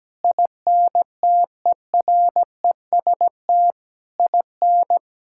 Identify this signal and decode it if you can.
Morse code, heard as a keyed tone, about 17 wpm, 700 Hz. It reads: INTEREST IN
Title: Sentences from Top 1000 Words - Review Part 5 - 17wpm